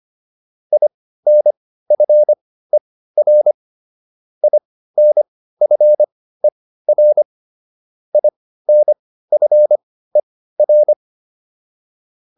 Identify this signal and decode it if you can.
Morse code, heard as a keyed tone, about 25 wpm, spaced as for 15 wpm, 600 Hz. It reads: INFER INFER INFER
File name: IN5 15